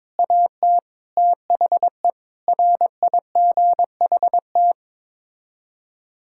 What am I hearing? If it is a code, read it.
Morse code, heard as a keyed tone, about 21 wpm, 700 Hz. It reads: AT THE RIGHT
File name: practice / 22wpm / Sets of 6 Words - 22wpm